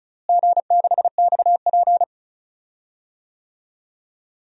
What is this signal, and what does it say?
Morse code, heard as a keyed tone, about 36 wpm, 700 Hz. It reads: G6XP
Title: DXCC-100 Call Signs - 4 Characters - Repeated 3x - 35wpm